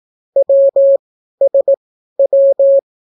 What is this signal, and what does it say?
Morse code, heard as a keyed tone, about 18 wpm, spaced as for 12 wpm, 550 Hz. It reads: WSW